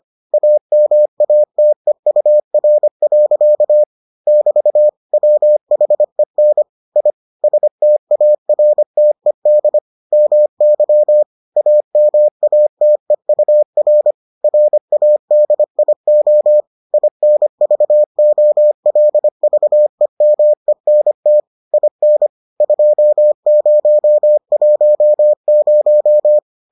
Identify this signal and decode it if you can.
Morse code, heard as a keyed tone, about 25 wpm, 600 Hz. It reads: AMATEUR. <BT> WHEN I STARTED MY AMATEUR RADIO INVOLVEMENT IN 2010